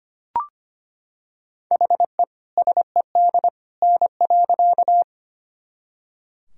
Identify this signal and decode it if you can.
Morse code, heard as a keyed tone, about 25 wpm, 700 Hz. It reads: HE SED N.
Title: Sentences from Top 100 Words and Abbreviations 25wpm